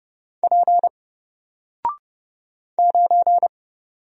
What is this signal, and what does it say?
Morse code, heard as a keyed tone, about 30 wpm, 700 Hz. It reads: P 9
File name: Single-Letter-Number-30wpm